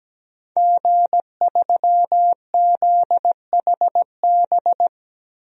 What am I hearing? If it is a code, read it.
Morse code, heard as a keyed tone, about 17 wpm, 700 Hz. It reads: G3ZHB
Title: DXCC-100 Call Signs - 5 Characters - Repeated 3x - 17wpm